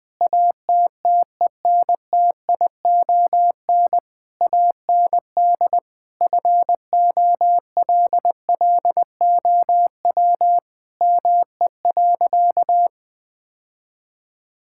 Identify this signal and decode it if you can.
Morse code, heard as a keyed tone, about 20 wpm, 700 Hz. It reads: ATTENTION AND FOLLOW ME.
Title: Sentences from Top 1000 Words - Review Part 6 - 20wpm